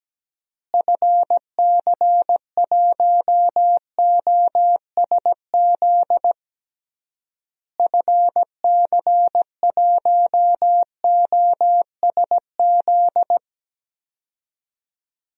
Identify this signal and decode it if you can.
Morse code, heard as a keyed tone, about 17 wpm, 700 Hz. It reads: FC1OSZ FC1OSZ
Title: DXCC-100 Call Signs - 6 Characters - Repeated 3x - 17wpm